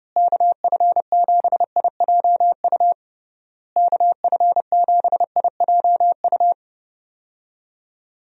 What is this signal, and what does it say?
Morse code, heard as a keyed tone, about 30 wpm, 700 Hz. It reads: KF7IJU KF7IJU
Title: USA Call Signs - 6 Characters - Repeated 3x - 30wpm